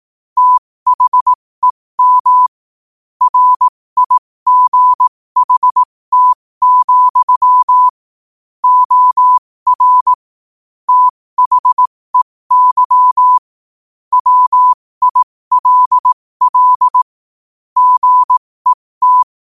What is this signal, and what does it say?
Morse code, heard as a keyed tone, about 18 wpm, spaced as for 15 wpm, 1000 Hz. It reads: THEM RIGHT, OR THEY WILL GET
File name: qotdfiles / 15wpm